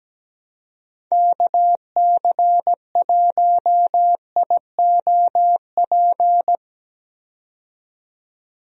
Morse code, heard as a keyed tone, about 17 wpm, 700 Hz. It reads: KC1IOP
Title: USA Call Signs - 6 Characters - 17wpm